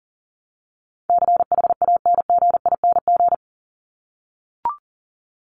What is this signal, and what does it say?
Morse code, heard as a keyed tone, about 40 wpm, 700 Hz. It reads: CHANGING
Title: ING Suffix - Top 100 Words with Base Words 40wpm